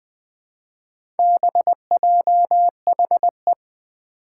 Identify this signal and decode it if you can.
Morse code, heard as a keyed tone, about 20 wpm, 700 Hz. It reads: BJHE
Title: Four Challenging Letter-Number Combinations 20wpm